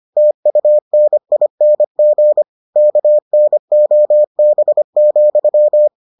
Morse code, heard as a keyed tone, about 25 wpm, 600 Hz. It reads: TUNING KNOB,